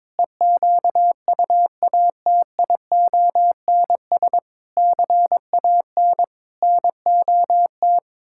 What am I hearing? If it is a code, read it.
Morse code, heard as a keyed tone, about 22 wpm, 700 Hz. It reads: EQUATIONS CAN NOT
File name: practice / 22wpm / Build A Phrase - 5-Word Phrases - 22wpm